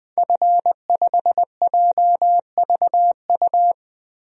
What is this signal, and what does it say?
Morse code, heard as a keyed tone, about 20 wpm, 700 Hz. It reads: F5JVU
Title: DXCC-100 Call Signs - 5 Characters - 20wpm